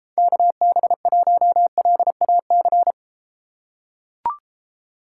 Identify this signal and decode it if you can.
Morse code, heard as a keyed tone, about 33 wpm, 700 Hz. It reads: KB1LAC